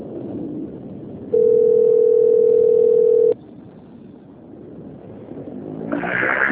Ringback tone (North America), 440 + 480 Hz.